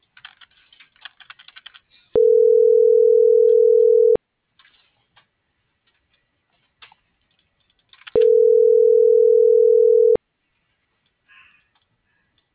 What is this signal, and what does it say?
Ringback tone (North America), 440 + 480 Hz.